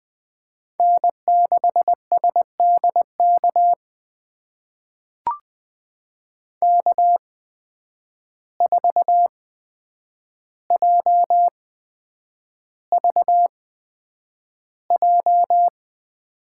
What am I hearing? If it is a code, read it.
Morse code, heard as a keyed tone, about 20 wpm, spaced as for 9 wpm, 700 Hz. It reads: N6SDK K4JVJ
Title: USA Call Signs - 5 Characters - 8x Character Spacing - 20wpm